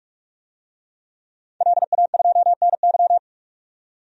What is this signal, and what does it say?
Morse code, heard as a keyed tone, about 46 wpm, 700 Hz. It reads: LA2NY
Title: DXCC-100 Call Signs - 5 Characters - 45wpm